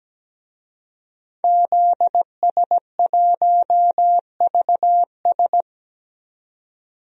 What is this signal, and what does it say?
Morse code, heard as a keyed tone, about 17 wpm, 700 Hz. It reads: ZS1VS
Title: DXCC-100 Call Signs - 5 Characters - 17wpm